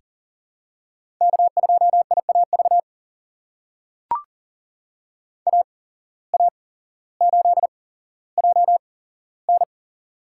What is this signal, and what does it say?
Morse code, heard as a keyed tone, about 37 wpm, spaced as for 17 wpm, 700 Hz. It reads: K2IAV AA8JN
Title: USA Call Signs - 5 Characters - 8x Character Spacing - 40wpm